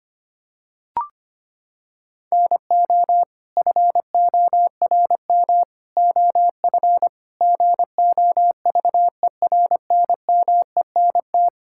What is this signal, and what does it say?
Morse code, heard as a keyed tone, about 25 wpm, 700 Hz. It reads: NO FORM OF GOVERNMENT
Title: Sentences from Top 1000 Words - Review Part 6 - 25wpm